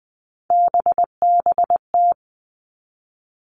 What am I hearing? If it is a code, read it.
Morse code, heard as a keyed tone, about 20 wpm, 700 Hz. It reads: BBT
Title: Three Challenging Letter-Number Combinations 20wpm